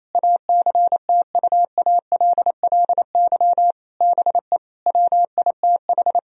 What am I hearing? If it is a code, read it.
Morse code, heard as a keyed tone, about 28 wpm, 700 Hz. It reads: ACTUALLY BE WITH
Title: Sentences from Top 1000 Words - Review Part 4 - 28wpm